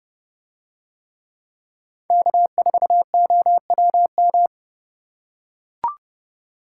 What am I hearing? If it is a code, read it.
Morse code, heard as a keyed tone, about 30 wpm, 700 Hz. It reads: K4OWM